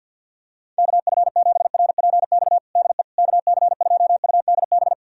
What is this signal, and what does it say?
Morse code, heard as a keyed tone, about 51 wpm, 700 Hz. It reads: KF7RPX DE KC2UDB
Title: USA Callsign DE Callsign - 6 Characters - 50wpm